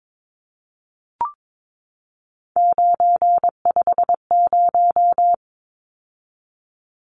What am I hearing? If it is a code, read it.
Morse code, heard as a keyed tone, about 22 wpm, 700 Hz. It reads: 950
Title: Numbers - 3-Digit 22wpm